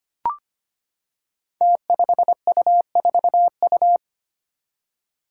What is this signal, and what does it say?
Morse code, heard as a keyed tone, about 25 wpm, 700 Hz. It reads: T5U4U
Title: Five Challenging Letter-Number Combinations 25wpm